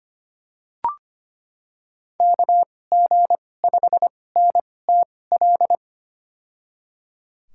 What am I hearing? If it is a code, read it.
Morse code, heard as a keyed tone, about 25 wpm, spaced as for 18 wpm, 700 Hz. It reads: KG5NTL
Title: USA Call Signs - 6 Characters - 2x Character Spacing - 25wpm